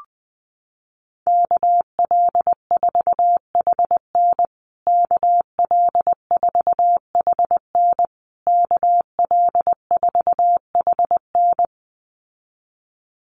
Morse code, heard as a keyed tone, about 20 wpm, 700 Hz. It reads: KL4HN KL4HN KL4HN
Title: USA Call Signs Repeated 3x 20wpm